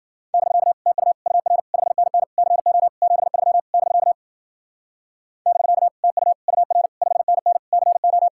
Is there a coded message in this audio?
Yes: Morse code from 0.3 to 8.3 s, heard as a keyed tone, about 53 wpm, 700 Hz. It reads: <BK> TU UR 5NN CO 73 <BK> <BK> TU UR 5NN CO